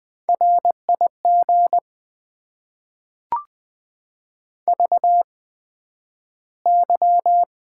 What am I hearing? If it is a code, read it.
Morse code, heard as a keyed tone, about 20 wpm, spaced as for 11 wpm, 700 Hz. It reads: RIG VY
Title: Top 90 QSO Elements - 8x Character Spacing - 20wpm